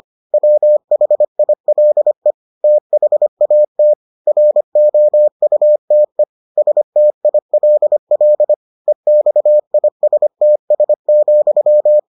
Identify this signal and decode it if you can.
Morse code, heard as a keyed tone, about 25 wpm, 600 Hz. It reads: WHILE THAT ROUTE STILL EXISTS,